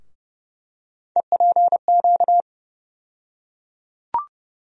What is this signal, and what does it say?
Morse code, heard as a keyed tone, about 31 wpm, 700 Hz. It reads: EPQ
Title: USA Call Sign Postfixes - 3 Letters 30wpm